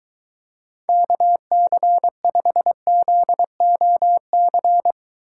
Morse code, heard as a keyed tone, about 22 wpm, 700 Hz. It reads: KC5ZOC